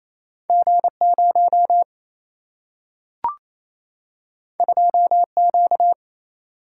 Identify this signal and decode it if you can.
Morse code, heard as a keyed tone, about 28 wpm, 700 Hz. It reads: G 0 2 Q
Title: Single-Letter Single-Number 28wpm